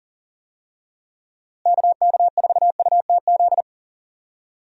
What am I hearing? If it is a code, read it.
Morse code, heard as a keyed tone, about 40 wpm, 700 Hz. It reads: KK4UTZ
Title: USA Call Signs - 6 Characters - Repeated 2x - 40wpm